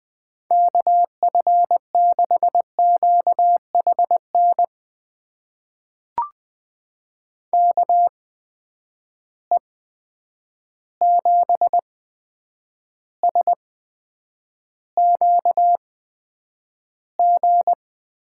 Morse code, heard as a keyed tone, about 20 wpm, spaced as for 9 wpm, 700 Hz. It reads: KF6QHN KE7SQG
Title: USA Call Signs - 6 Characters - 8x Character Spacing - 20wpm